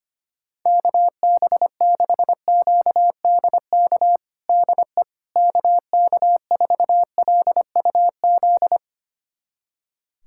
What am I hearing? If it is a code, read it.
Morse code, heard as a keyed tone, about 25 wpm, 700 Hz. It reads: KB6QDK DE KK4LUZ
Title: USA Callsign DE Callsign 25wpm